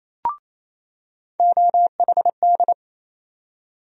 Morse code, heard as a keyed tone, about 28 wpm, 700 Hz. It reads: OHD
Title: USA Call Sign Postfixes - 3 Letters 28wpm